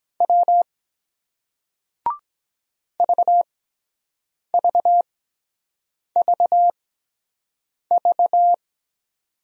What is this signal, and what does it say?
Morse code, heard as a keyed tone, about 23 wpm, spaced as for 7 wpm, 700 Hz. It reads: W VVVV